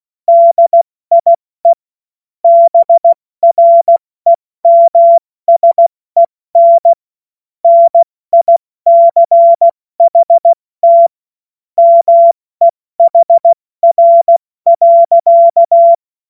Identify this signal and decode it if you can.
Morse code, heard as a keyed tone, about 16 wpm, 680 Hz. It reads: DIE BREMSEN NICHT MEHR.
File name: Morsekurs / Witze / Witz 14. Dezember 2021, 70 F80